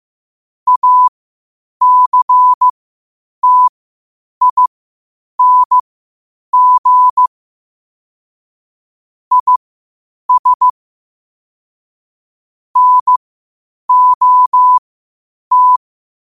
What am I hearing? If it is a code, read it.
Morse code, heard as a keyed tone, about 15 wpm, spaced as for 8 wpm, 1000 Hz. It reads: ACTING IS NOT
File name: qotdfiles / 8wpm